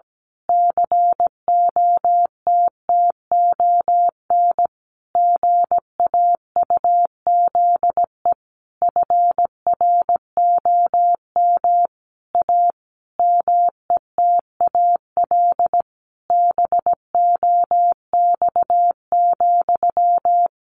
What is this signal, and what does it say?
Morse code, heard as a keyed tone, about 17 wpm, 700 Hz. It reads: COTTON GAUZE FROM A METAL BOX,
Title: A Sound Of Thunder 17wpm